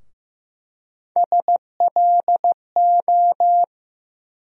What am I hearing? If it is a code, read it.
Morse code, heard as a keyed tone, about 15 wpm, 700 Hz. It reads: SLO
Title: USA Call Sign Postfixes - 3 Letters 15wpm